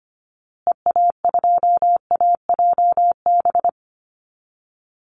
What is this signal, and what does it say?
Morse code, heard as a keyed tone, about 25 wpm, 700 Hz. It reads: EA2AJB
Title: Call Signs Repeated 2x - 25wpm